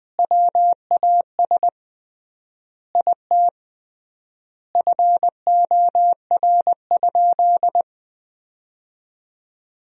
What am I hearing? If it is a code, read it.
Morse code, heard as a keyed tone, about 21 wpm, 700 Hz. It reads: WAS IT FOR?
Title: 3x Word-Spacing - Sentences from Top 100 Words - 20wpm